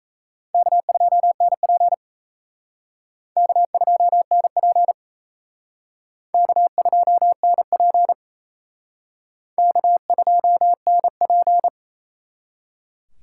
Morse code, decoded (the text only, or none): K2NP K2NP K2NP K2NP